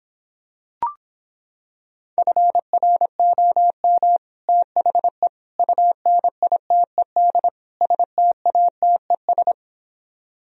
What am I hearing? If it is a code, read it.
Morse code, heard as a keyed tone, about 26 wpm, 700 Hz. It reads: FROM THE UNITED STATES